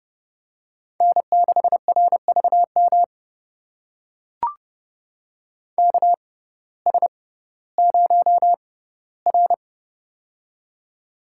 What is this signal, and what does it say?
Morse code, heard as a keyed tone, about 30 wpm, spaced as for 17 wpm, 700 Hz. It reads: N6RVM KS0R
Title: USA Call Signs - 6x Character Spacing - 30wpm